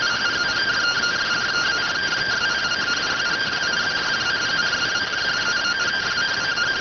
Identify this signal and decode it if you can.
Two-tone teleprinter signal, about 170 Hz shift.